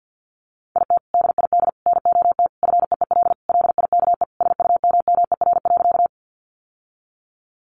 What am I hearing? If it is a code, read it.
Morse code, heard as a keyed tone, about 50 wpm, 700 Hz. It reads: IT DID NOT FEEL LIKE SUMMER.
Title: Sentences from Top 1100 Words 50wpm